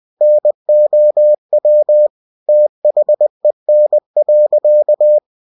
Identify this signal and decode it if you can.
Morse code, heard as a keyed tone, about 20 wpm, 600 Hz. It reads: NOW THEN.